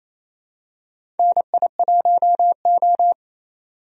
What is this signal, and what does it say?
Morse code, heard as a keyed tone, about 28 wpm, 700 Hz. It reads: NI1O
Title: USA Call Signs - 4 Characters - 28wpm